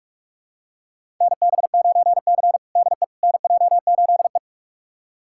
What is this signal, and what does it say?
Morse code, heard as a keyed tone, about 42 wpm, 700 Hz. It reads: NB9C DE NJ8E
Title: USA Callsign DE Callsign - 4 Characters - 45wpm